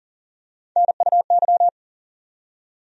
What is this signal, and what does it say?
Morse code, heard as a keyed tone, about 40 wpm, 700 Hz. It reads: NUY
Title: USA Call Sign Postfixes - 3 Letters 40wpm